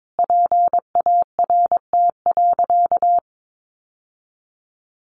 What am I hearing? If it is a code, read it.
Morse code, heard as a keyed tone, about 22 wpm, 700 Hz. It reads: PART.
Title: Sentences from Top 1200 Words 22wpm